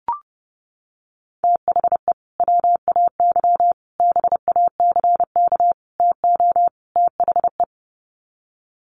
Morse code, heard as a keyed tone, about 30 wpm, 700 Hz. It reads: THE WAY BACK TO THE
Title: Sets of 5 Words Encore2 30wpm